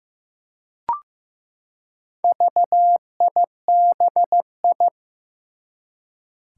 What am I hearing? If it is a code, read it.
Morse code, heard as a keyed tone, about 15 wpm, 700 Hz. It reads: VIBI
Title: Four Challenging Letter-Number Combinations 15wpm